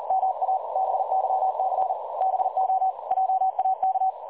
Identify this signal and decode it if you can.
Morse code, heard as a keyed tone, about 39 wpm, 760 Hz, with noise. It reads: MA BYC BO JAK
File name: dlazaawansowanych / nagraniazpasma / websdr recording 3565.0kHz